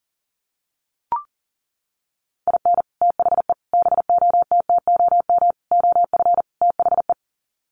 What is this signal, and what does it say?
Morse code, heard as a keyed tone, about 40 wpm, 700 Hz. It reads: IN THE BOTTOM OF THE
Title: Sets of 5 Words Encore 40wpm